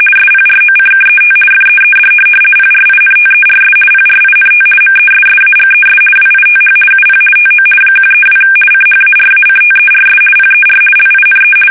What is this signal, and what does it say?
RTTY, about 50 baud. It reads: SPROCKET FEED PRINTERS PRINT MESSAGES WITH UP TO 132 CHARACTERS PER LINE
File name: fsk850-50B / teletype-corp